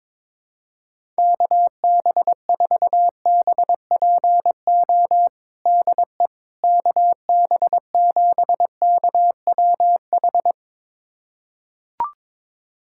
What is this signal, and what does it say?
Morse code, heard as a keyed tone, about 22 wpm, 700 Hz. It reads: KB4BPO DE KB7KWH
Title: USA Callsign DE Callsign - 6 Characters - 22wpm